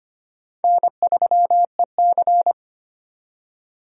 Morse code, heard as a keyed tone, about 30 wpm, 700 Hz. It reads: N3EC
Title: Speed Racing - USA 1x2 Call Signs 25wpm